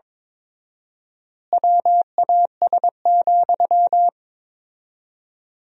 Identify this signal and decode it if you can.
Morse code, heard as a keyed tone, about 22 wpm, 690 Hz. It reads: WAS,
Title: 4x Word-Spacing - Sentences from Top 400 Words - 22wpm